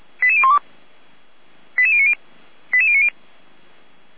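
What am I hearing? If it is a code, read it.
Five-tone selective call (ZVEI), heard as quick runs of five tones, tones of about 70 ms. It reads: 88012 88090 88090
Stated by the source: ZVEI five-tone call 88192 received by SDR, recorded by Feuerwehr Orlamünde fire brigade.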